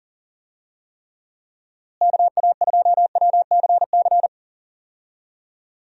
Morse code, heard as a keyed tone, about 40 wpm, 700 Hz. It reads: KA2WCC
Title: USA Call Signs - 6 Characters - Repeated 3x - 40wpm